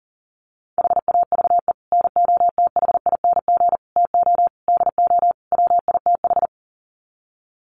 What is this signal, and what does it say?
Morse code, heard as a keyed tone, about 40 wpm, 700 Hz. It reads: HAVE NOTHING TO DO WITH
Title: Sets of 5 Words 40wpm